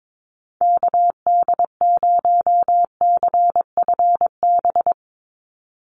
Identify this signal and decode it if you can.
Morse code, heard as a keyed tone, about 22 wpm, 700 Hz. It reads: KD0CFB
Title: USA Call Signs 22wpm